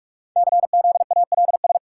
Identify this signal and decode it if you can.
Morse code, heard as a keyed tone, about 44 wpm, 700 Hz. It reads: CZALS
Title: Code Groups - 5 Letters - 45wpm